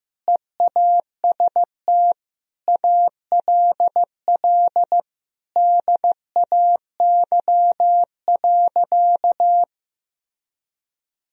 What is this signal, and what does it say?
Morse code, heard as a keyed tone, about 15 wpm, 700 Hz. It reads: EAST ALL DAY.
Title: Sentences from Top 1000 Words - Review Part 3 - 15wpm